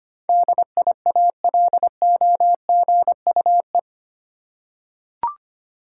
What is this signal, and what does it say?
Morse code, heard as a keyed tone, about 25 wpm, 700 Hz. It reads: DIALOGUE
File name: 8 Letter Words - First 3 Letters Given - 25wpm